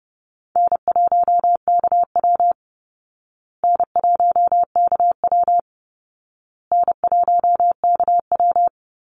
Morse code, heard as a keyed tone, about 30 wpm, 700 Hz. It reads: N1KW N1KW N1KW
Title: Call Signs Repeated 2x - 30wpm